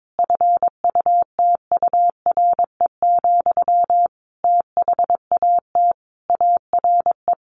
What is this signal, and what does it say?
Morse code, heard as a keyed tone, about 22 wpm, 700 Hz. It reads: FUTURE, THAT ARE
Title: A Sound Of Thunder 22wpm